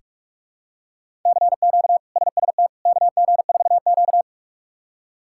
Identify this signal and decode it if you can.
Morse code, heard as a keyed tone, about 47 wpm, 700 Hz. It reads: CQ SST KG4Q
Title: K1USN SST QSO Practice - 45wpm